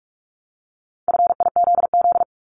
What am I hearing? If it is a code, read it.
Morse code, heard as a keyed tone, about 46 wpm, 700 Hz. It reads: FIZZ
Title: Multi-letter Phonemes Repeated 3x 45wpm